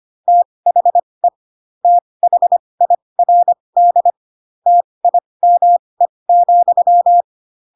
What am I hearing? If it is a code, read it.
Morse code, heard as a keyed tone, about 25 wpm, spaced as for 20 wpm, 700 Hz. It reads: THE THIRD TIME,
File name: Pinocchio / Pinocchio ch2 15 25 ik1hge